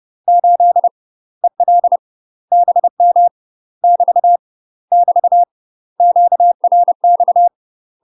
Morse code, heard as a keyed tone, about 30 wpm, 700 Hz. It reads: 8 EL BM <BT> <BT> QRX
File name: UK MASTER QSOs TRAINING / 30WPM / UK MASTER QSOs 0001